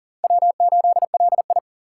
Morse code, heard as a keyed tone, about 40 wpm, 700 Hz. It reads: W8LI
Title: USA Call Signs - 4 Characters - 4x Character Spacing - 40wpm